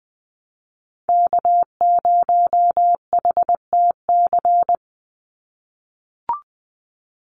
Morse code, heard as a keyed tone, about 20 wpm, 700 Hz. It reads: K0HTC
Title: USA Call Signs 20wpm